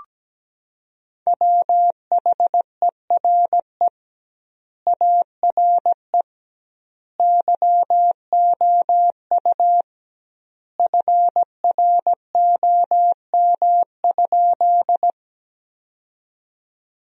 Morse code, heard as a keyed tone, about 17 wpm, 700 Hz. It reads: WHERE ARE YOU FROM?
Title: 2x Word-Spacing - Sentences from Top 200 Words - 17wpm